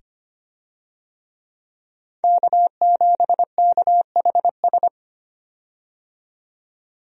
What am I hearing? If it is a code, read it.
Morse code, heard as a keyed tone, about 25 wpm, 700 Hz. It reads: K7KHS
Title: USA Call Signs - 5 Characters - 25wpm